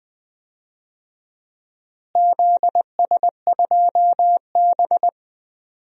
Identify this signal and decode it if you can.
Morse code, heard as a keyed tone, about 20 wpm, 700 Hz. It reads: ZS2B
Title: DXCC-100 Call Signs - 4 Characters - Repeated 2x - 20wpm